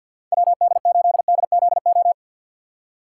Morse code, heard as a keyed tone, about 53 wpm, 700 Hz. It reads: WD8DZO
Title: USA Call Signs - 6 Characters - 8x Character Spacing - 50wpm